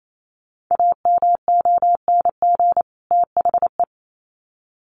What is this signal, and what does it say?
Morse code, heard as a keyed tone, about 28 wpm, 700 Hz. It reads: AMONG THE
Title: Sets of 2 Words 28wpm